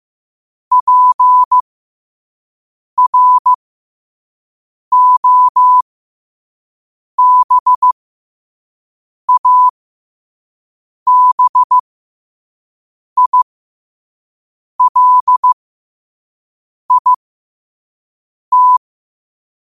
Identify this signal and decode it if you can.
Morse code, heard as a keyed tone, about 15 wpm, spaced as for 5 wpm, 1000 Hz. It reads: PROBABILIT
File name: qotdfiles / 5wpm